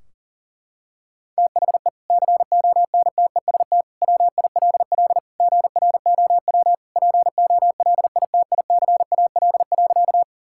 Morse code, heard as a keyed tone, about 40 wpm, 700 Hz. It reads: THE CONTEST WILL GROW POLITICAL.
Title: Sentences from Top 1600 Words 40wpm